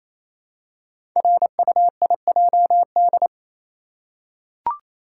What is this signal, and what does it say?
Morse code, heard as a keyed tone, about 28 wpm, 700 Hz. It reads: RUIJD
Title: Code Groups - 5 Letters - 28wpm